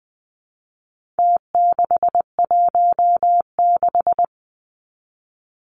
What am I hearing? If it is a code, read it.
Morse code, heard as a keyed tone, about 20 wpm, 700 Hz. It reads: T616
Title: Numbers - Contest Style - 4-Digit 20wpm